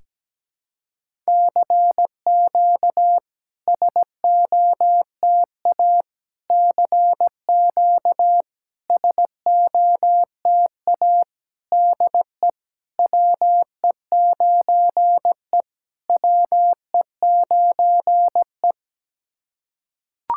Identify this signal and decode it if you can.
Morse code, heard as a keyed tone, about 17 wpm, 700 Hz. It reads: CQ SOTA CQ SOTA DE WE9E WE9E
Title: SOTA QSO Practice 17wpm